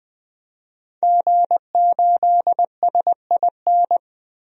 Morse code, heard as a keyed tone, about 20 wpm, 700 Hz. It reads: G8SIN
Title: DXCC-100 Call Signs - 5 Characters - 20wpm